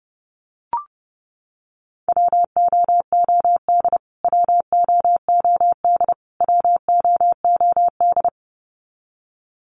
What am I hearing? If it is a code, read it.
Morse code, heard as a keyed tone, about 30 wpm, 700 Hz. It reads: WOOD WOOD WOOD
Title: Top 300 Words Repeated 30wpm